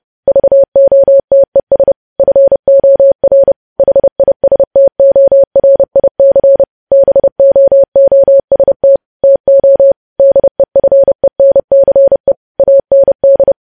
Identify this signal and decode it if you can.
Morse code, heard as a keyed tone, about 30 wpm, 550 Hz. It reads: VOTES FOR HISTORIC BOOST TO DEFENCE AND
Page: news headlines in morse code at 30 wpm